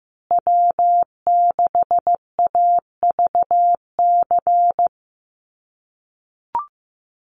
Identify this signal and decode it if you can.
Morse code, heard as a keyed tone, about 15 wpm, 700 Hz. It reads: W6AVC
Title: USA Call Signs Repeated 3x 15wpm